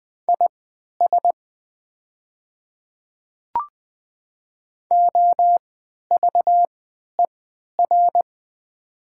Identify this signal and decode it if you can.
Morse code, heard as a keyed tone, about 20 wpm, spaced as for 11 wpm, 700 Hz. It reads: IS OVER
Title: 3x Character Spacing - Top 100 Words - 20wpm